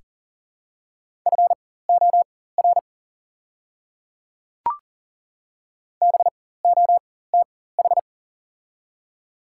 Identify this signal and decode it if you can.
Morse code, heard as a keyed tone, about 40 wpm, spaced as for 19 wpm, 700 Hz. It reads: FOR BOTH
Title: 4x Character Spacing - Top 300 Words - 40wpm